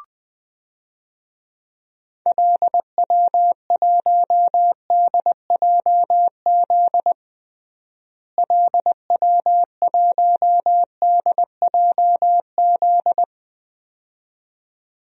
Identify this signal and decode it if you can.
Morse code, heard as a keyed tone, about 20 wpm, 700 Hz. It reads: LW1DJZ LW1DJZ
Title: DXCC-100 Call Signs - 6 Characters - Repeated 2x - 20wpm